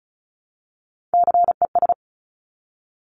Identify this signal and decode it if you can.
Morse code, heard as a keyed tone, about 36 wpm, 700 Hz. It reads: CES
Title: Top Three Letter Combinations in Words 35wpm